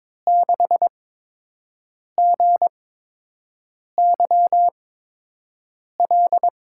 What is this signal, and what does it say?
Morse code, heard as a keyed tone, about 22 wpm, 700 Hz. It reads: 6 G Y L
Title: USA Call Signs - 6 Characters - 8x Character Spacing - 22wpm